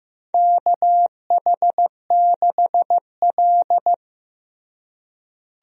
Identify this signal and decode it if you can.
Morse code, heard as a keyed tone, about 15 wpm, 700 Hz. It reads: KH6L
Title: DXCC-100 Call Signs - 4 Characters - Repeated 3x - 15wpm